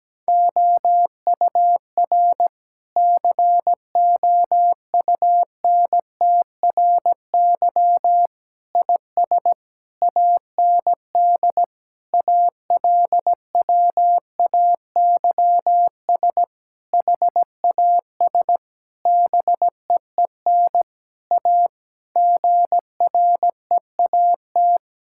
Morse code, heard as a keyed tone, about 17 wpm, 700 Hz. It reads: OUR COUNTRY IS AND ALWAYS HAS BEEN A GREAT
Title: Sentences from Top 1000 Words - Review Part 6 - 17wpm